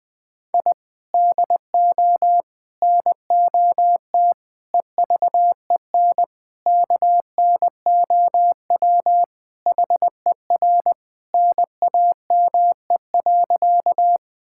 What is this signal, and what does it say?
Morse code, heard as a keyed tone, about 19 wpm, 700 Hz. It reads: I DO NOT EVEN KNOW HER NAME.
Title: Sentences with 5-Word Phrases - 20wpm